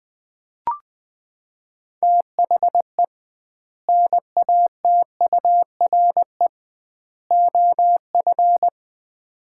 Morse code, heard as a keyed tone, about 20 wpm, 700 Hz. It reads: THE NATURE OF
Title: 2x Word-Spacing - Sets of 3 Words - 20wpm